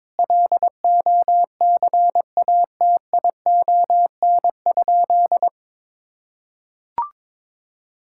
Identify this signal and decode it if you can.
Morse code, heard as a keyed tone, about 22 wpm, 700 Hz. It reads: LOCATION?
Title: Sentences from Top 1000 Words - Review Part 6 - 22wpm